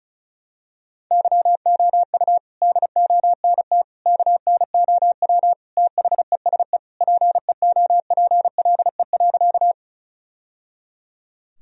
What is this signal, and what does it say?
Morse code, heard as a keyed tone, about 36 wpm, 700 Hz. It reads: YOU DONT KNOW THESE PEOPLE.
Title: Sentences from Top 1000 Words - Review Part 1 - 35wpm